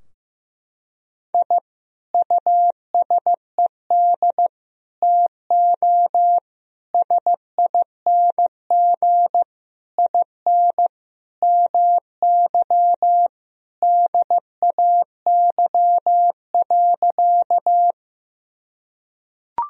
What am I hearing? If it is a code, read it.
Morse code, heard as a keyed tone, about 15 wpm, 700 Hz. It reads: I USED TO SING IN MY DAY.
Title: Sentences from Top 1000 Words - Review Part 2 - 15wpm